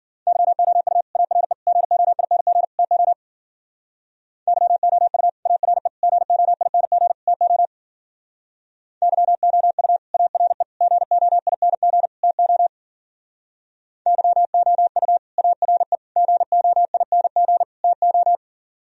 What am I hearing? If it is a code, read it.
Morse code, heard as a keyed tone, about 52 wpm, 700 Hz. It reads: YOU ARE GOING TO YOU ARE GOING TO YOU ARE GOING TO YOU ARE GOING TO